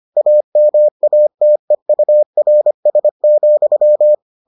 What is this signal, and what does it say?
Morse code, heard as a keyed tone, about 25 wpm, 600 Hz. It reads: AMATEURS,